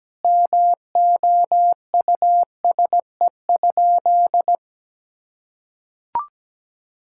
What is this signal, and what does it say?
Morse code, heard as a keyed tone, about 17 wpm, 700 Hz. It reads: MOUSE?
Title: Sentences from Top 1000 Words - Review Part 5 - 17wpm